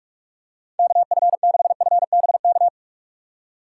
Morse code, heard as a keyed tone, about 46 wpm, 700 Hz. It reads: KF6FBK